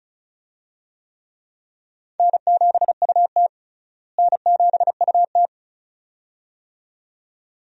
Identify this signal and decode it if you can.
Morse code, heard as a keyed tone, about 36 wpm, 700 Hz. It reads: N7UT N7UT
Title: USA Call Signs - 4 Characters - Repeated 2x - 35wpm